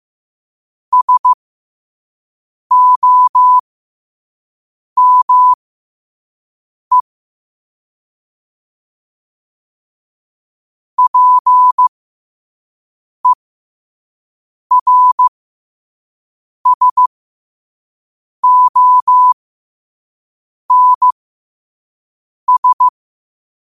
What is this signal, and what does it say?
Morse code, heard as a keyed tone, about 15 wpm, spaced as for 5 wpm, 1000 Hz. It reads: SOME PERSONS
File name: qotdfiles / 5wpm